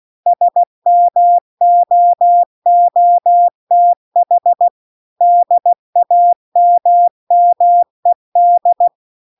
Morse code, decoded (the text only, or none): SMOOTH DAMMED